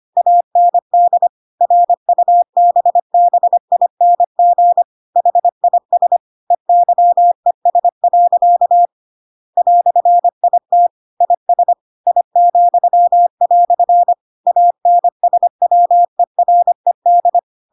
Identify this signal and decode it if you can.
Morse code, heard as a keyed tone, about 25 wpm, 700 Hz. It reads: AND RUBBING HIS EYES. "IT IS I," ANSWERED